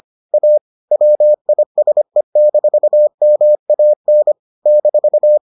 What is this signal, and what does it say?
Morse code, heard as a keyed tone, about 25 wpm, 600 Hz. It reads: A WISE-MAN -